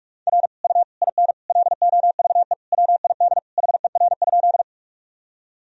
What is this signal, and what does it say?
Morse code, heard as a keyed tone, about 46 wpm, 700 Hz. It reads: R U IN LOVE WID HER?
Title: Sentences from Top 400 Words and Abbreviations 45wpm